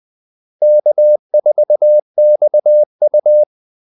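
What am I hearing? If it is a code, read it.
Morse code, heard as a keyed tone, about 20 wpm, 600 Hz. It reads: K4XU
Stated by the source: Morse call-sign practice at 20 wpm.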